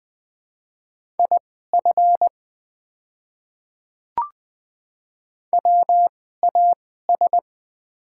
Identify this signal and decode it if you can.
Morse code, heard as a keyed tone, about 20 wpm, spaced as for 14 wpm, 700 Hz. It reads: IF WAS
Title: 2x Character Spacing - Top 200 Words - 20wpm